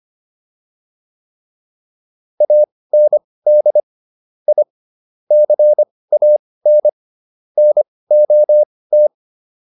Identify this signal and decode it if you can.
Morse code, heard as a keyed tone, about 25 wpm, spaced as for 18 wpm, 600 Hz. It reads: AND I CAN NOT